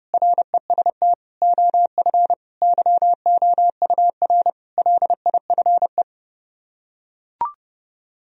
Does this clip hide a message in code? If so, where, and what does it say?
Yes: Morse code from 0.1 to 6.0 s, heard as a keyed tone, about 30 wpm, 700 Hz. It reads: REST OF YOUR LIFE